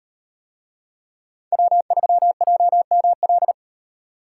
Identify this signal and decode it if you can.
Morse code, heard as a keyed tone, about 38 wpm, 700 Hz. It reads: W3JML